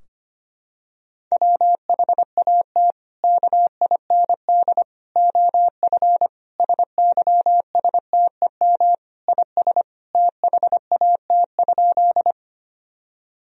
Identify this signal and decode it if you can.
Morse code, heard as a keyed tone, about 25 wpm, 700 Hz. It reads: WHAT KIND OF SYSTEM IS THAT?
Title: Sentences from Top 1000 Words - Review Part 3 - 25wpm